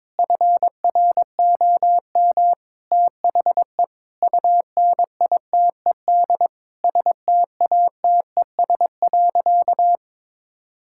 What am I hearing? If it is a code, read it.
Morse code, heard as a keyed tone, about 21 wpm, 700 Hz. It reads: FROM THE UNITED STATES.
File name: practice / 22wpm / Sentences with 4-Word Phrases - 22wpm